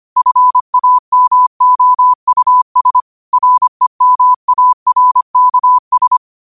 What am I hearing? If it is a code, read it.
Morse code, heard as a keyed tone, about 25 wpm, 1000 Hz. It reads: FAMOUS REMARKS
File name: qotdfiles / 25wpm